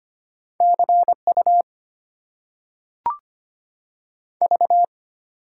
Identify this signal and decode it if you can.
Morse code, heard as a keyed tone, about 25 wpm, 700 Hz. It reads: CU V
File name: Top 90 QSO Elements - 8x Character Spacing - 25wpm